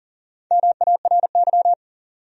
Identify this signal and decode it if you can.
Morse code, heard as a keyed tone, about 37 wpm, 700 Hz. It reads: MARY
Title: USA Common Names - 40wpm